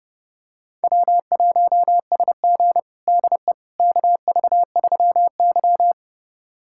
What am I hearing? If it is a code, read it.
Morse code, heard as a keyed tone, about 30 wpm, 700 Hz. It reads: W1SG DE KV3Y
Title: USA Callsign DE Callsign - 4 Characters - 30wpm